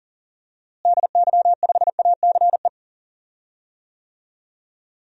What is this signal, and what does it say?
Morse code, heard as a keyed tone, about 40 wpm, 700 Hz. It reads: DY5ACE
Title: DXCC-100 Call Signs - 6 Characters - Repeated 2x - 40wpm